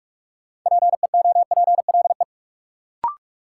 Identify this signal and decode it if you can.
Morse code, heard as a keyed tone, about 42 wpm, 700 Hz. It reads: PEOPLE
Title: Sentences from Top 200 Words - New Word Drill - 45wpm